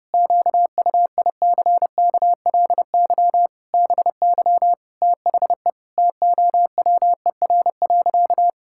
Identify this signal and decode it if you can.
Morse code, heard as a keyed tone, about 32 wpm, 700 Hz. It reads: QUICKLY BY THE TOWER.
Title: Pangram Challenge v2 - 30wpm